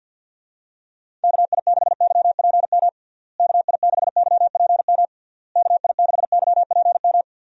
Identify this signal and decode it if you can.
Morse code, heard as a keyed tone, about 49 wpm, 700 Hz. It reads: KI6YPM KI6YPM KI6YPM
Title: USA Call Signs - 6 Characters - Repeated 3x - 50wpm